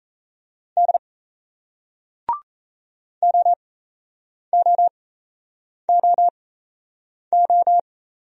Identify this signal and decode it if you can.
Morse code, heard as a keyed tone, about 35 wpm, spaced as for 8 wpm, 700 Hz. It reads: D OOOO